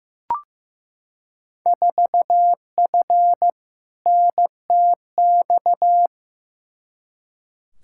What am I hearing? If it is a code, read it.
Morse code, heard as a keyed tone, about 15 wpm, 700 Hz. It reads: 4F NTX
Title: Field Day Exchanges Encore - 15wpm